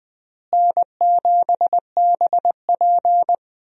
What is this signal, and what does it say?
Morse code, heard as a keyed tone, about 20 wpm, 700 Hz. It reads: N7BP
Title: USA Call Signs - 4 Characters - Repeated 2x - 20wpm